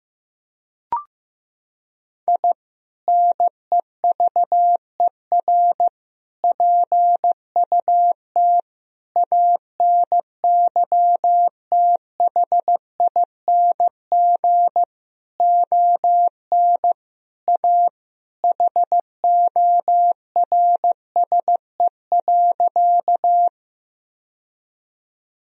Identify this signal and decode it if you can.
Morse code, heard as a keyed tone, about 15 wpm, 700 Hz. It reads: I NEVER PUT ANYTHING ON A HORSE.
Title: Sentences from Top 1000 Words - Review Part 1 - 15wpm